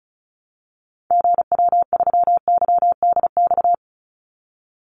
Morse code, heard as a keyed tone, about 34 wpm, 700 Hz. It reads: GW3YDX
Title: Call Signs - 35wpm